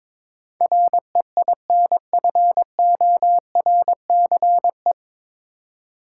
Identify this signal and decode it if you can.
Morse code, heard as a keyed tone, about 22 wpm, 700 Hz. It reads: REINFORCE
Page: Morse Code Ninja practice 22wpm